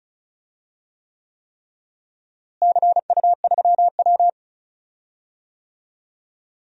Morse code, heard as a keyed tone, about 34 wpm, 700 Hz. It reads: CU3W
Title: DXCC-100 Call Signs - 4 Characters - Repeated 3x - 35wpm